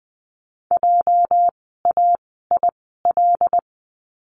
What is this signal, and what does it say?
Morse code, heard as a keyed tone, about 20 wpm, 700 Hz. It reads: J A I L
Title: 206 Lesson 33f - Letter j - Words 20wpm